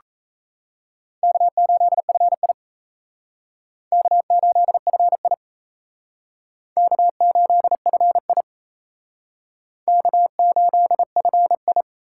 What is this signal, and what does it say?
Morse code, heard as a keyed tone, about 36 wpm, 700 Hz. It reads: K8FI K8FI K8FI K8FI